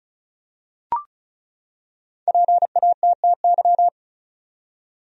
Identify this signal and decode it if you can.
Morse code, heard as a keyed tone, about 36 wpm, 700 Hz. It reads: PATTY
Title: USA Common Names - 35wpm